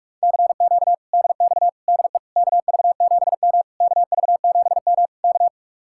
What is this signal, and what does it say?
Morse code, heard as a keyed tone, about 42 wpm, 700 Hz. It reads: CQ DX DE KV7M KV7M K